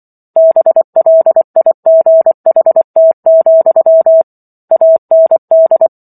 Morse code, heard as a keyed tone, about 24 wpm, 650 Hz. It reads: BLIGHT, AND
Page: War of the Worlds - 15-Chapter 15 - 24 WPM